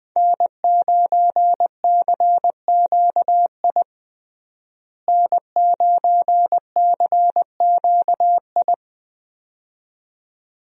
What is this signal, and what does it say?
Morse code, heard as a keyed tone, about 20 wpm, 700 Hz. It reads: N9CQI N9CQI
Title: USA Call Signs - 5 Characters - Repeated 2x - 20wpm